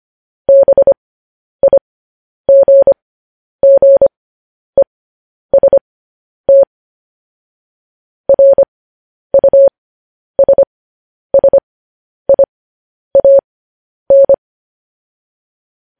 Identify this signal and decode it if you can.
Morse code, heard as a keyed tone, about 25 wpm, spaced as for 10 wpm, 550 Hz. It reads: BIGGEST RUSSIAN
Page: news headlines in morse code at 10 wpm